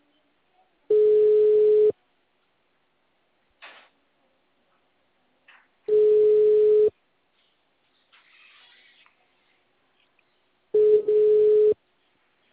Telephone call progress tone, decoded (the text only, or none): ringback tone